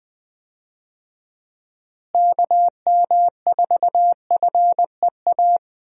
Morse code, heard as a keyed tone, about 21 wpm, 700 Hz. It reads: KM4FEA
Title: USA Call Signs - 6 Characters - Repeated 3x - 20wpm